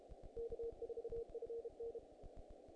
Morse code, heard as a keyed tone, about 30 wpm, 460 Hz. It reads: K4FN
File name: contests / naqp-cw-k5tr-op-k5pi- -1900